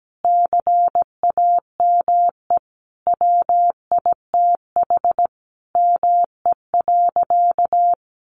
Morse code, heard as a keyed tone, about 17 wpm, 700 Hz. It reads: CAME WITH ME.
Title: Sentences from Top 1300 Words 17wpm